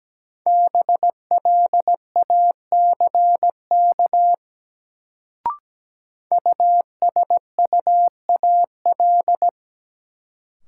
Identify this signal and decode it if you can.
Morse code, heard as a keyed tone, about 17 wpm, 700 Hz. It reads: BLACK USUAL
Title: Sentences from Top 400 Words - New Word Drill - 17wpm